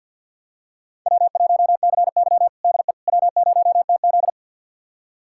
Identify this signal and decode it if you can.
Morse code, heard as a keyed tone, about 48 wpm, 700 Hz. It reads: W1CY DE W0TZ
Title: USA Callsign DE Callsign - 4 Characters - 50wpm